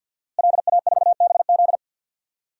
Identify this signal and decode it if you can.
Morse code, heard as a keyed tone, about 51 wpm, 700 Hz. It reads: LA4BZ
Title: DXCC-100 Call Signs - 5 Characters - Repeated 2x - 50wpm